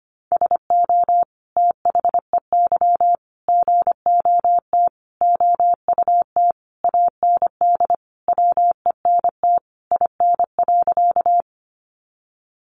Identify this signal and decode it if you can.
Morse code, heard as a keyed tone, about 25 wpm, 700 Hz. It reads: SO THEY GOT OUT AND WENT IN.
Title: Sentences from Top 300 Words 25wpm